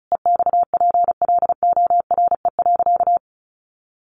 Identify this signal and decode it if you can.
Morse code, heard as a keyed tone, about 34 wpm, 700 Hz. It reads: EXPLORE.
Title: Sentences from Top 1100 Words 35wpm